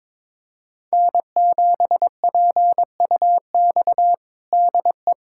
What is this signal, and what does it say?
Morse code, heard as a keyed tone, about 22 wpm, 700 Hz. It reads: N7PUX DE
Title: USA Callsign DE Callsign - 5 Characters - 22wpm